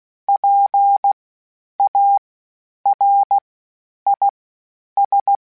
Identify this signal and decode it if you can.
Morse code, heard as a keyed tone, about 16 wpm, spaced as for 9 wpm, 800 Hz. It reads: PARIS
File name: PARIS Calibration / PARIS mp3s / PARIS@10-@15